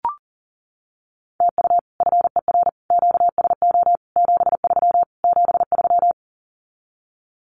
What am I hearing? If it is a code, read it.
Morse code, heard as a keyed tone, about 40 wpm, 700 Hz. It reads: TU FER QSO 73 73
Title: Common QSO Phrases Encore 40wpm